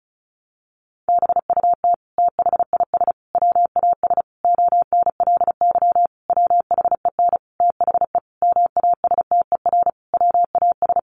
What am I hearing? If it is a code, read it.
Morse code, heard as a keyed tone, about 34 wpm, 700 Hz. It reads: BUT THIS WAS ONLY WHEN THE MASTER WAS
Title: Sentences from Top 1300 Words 35wpm